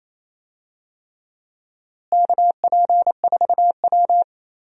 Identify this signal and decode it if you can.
Morse code, heard as a keyed tone, about 28 wpm, 700 Hz. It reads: KP4W
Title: DXCC-100 Call Signs - 4 Characters - Repeated 3x - 28wpm